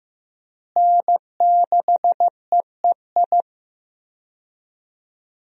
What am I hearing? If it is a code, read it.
Morse code, heard as a keyed tone, about 15 wpm, 700 Hz. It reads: N6EEI
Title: USA Call Signs - 5 Characters - Repeated 2x - 15wpm